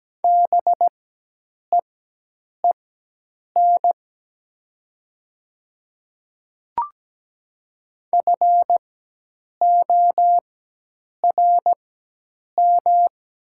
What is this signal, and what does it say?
Morse code, heard as a keyed tone, about 17 wpm, spaced as for 8 wpm, 700 Hz. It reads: BEEN FORM
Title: 4x Character Spacing - Top 300 Words - 17wpm